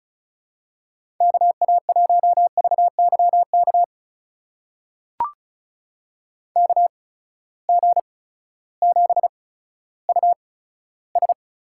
Morse code, heard as a keyed tone, about 36 wpm, spaced as for 16 wpm, 700 Hz. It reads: KA1VYK KG7US
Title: USA Call Signs - 6 Characters - 8x Character Spacing - 35wpm